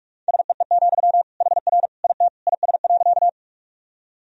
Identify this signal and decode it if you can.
Morse code, heard as a keyed tone, about 47 wpm, 700 Hz. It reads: SEE, HR IT IS.
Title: Sentences from Top 200 Words and Abbreviations 45wpm